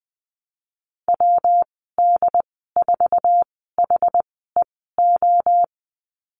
Morse code, heard as a keyed tone, about 20 wpm, spaced as for 14 wpm, 700 Hz. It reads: WD4HEO
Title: 140 Lesson 23g - Letter f - Call Signs 20wpm